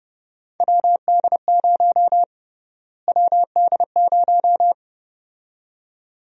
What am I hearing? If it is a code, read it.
Morse code, heard as a keyed tone, about 30 wpm, 700 Hz. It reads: WD0 WD0
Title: USA Call Sign Prefixes - Repeated 2x - 30wpm